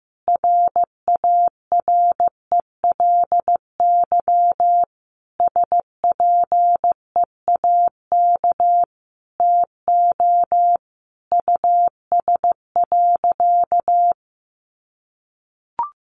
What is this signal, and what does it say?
Morse code, heard as a keyed tone, about 14 wpm, 700 Hz. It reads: RARELY SPEAK TO US.
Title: Sentences from Top 1400 Words 15wpm